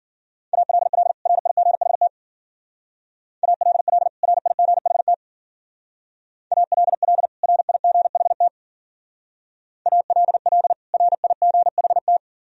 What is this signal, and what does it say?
Morse code, heard as a keyed tone, about 51 wpm, 700 Hz. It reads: ALL RIGHT ALL RIGHT ALL RIGHT ALL RIGHT